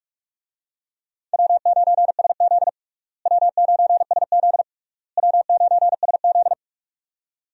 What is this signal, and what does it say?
Morse code, heard as a keyed tone, about 46 wpm, 700 Hz. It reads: W9SZ W9SZ W9SZ
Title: USA Call Signs - 4 Characters - Repeated 3x - 45wpm